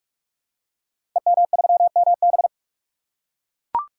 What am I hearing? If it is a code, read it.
Morse code, heard as a keyed tone, about 44 wpm, 700 Hz. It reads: EM3MB
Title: Code Groups - 5 Letter:Numbers - 45wpm